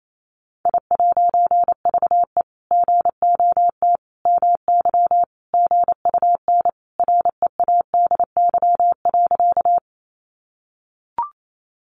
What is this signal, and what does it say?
Morse code, heard as a keyed tone, about 28 wpm, 700 Hz. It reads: I'VE GOT MY GUN READY.
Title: A Sound Of Thunder 28wpm